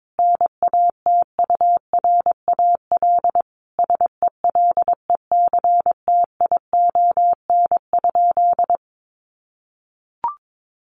Morse code, heard as a keyed tone, about 22 wpm, 700 Hz. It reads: NATURAL SELECTION?
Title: Sentences from Top 1400 Words 22wpm